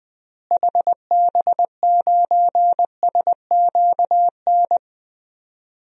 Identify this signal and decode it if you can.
Morse code, heard as a keyed tone, about 20 wpm, 700 Hz. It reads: HB9SQN
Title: DXCC-100 Call Signs - 6 Characters - Repeated 2x - 20wpm